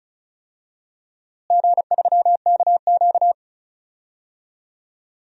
Morse code, heard as a keyed tone, about 36 wpm, 700 Hz. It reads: G3KQ
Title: DXCC-100 Call Signs - 4 Characters - Repeated 2x - 35wpm